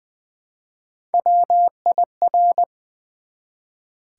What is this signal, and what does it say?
Morse code, heard as a keyed tone, about 20 wpm, 700 Hz. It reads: WIR
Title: Build A Word - 1 Letter At A Time - 3 to 5-Letter Words - 20wpm